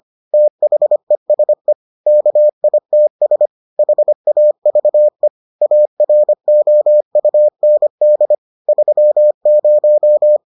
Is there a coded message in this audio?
Yes: Morse code from 0.3 to 10.4 s, heard as a keyed tone, about 25 wpm, 600 Hz. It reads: THESE KITS HAVE AROUND 30